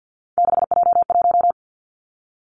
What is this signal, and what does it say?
Morse code, heard as a keyed tone, about 50 wpm, 700 Hz. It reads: 6J1
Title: Three Challenging Letter-Number Combinations 50wpm